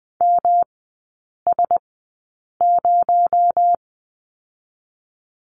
Morse code, heard as a keyed tone, about 20 wpm, 700 Hz. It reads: M S 0
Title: 218 Lesson 35e - Number 0 - Three Characters 20wpm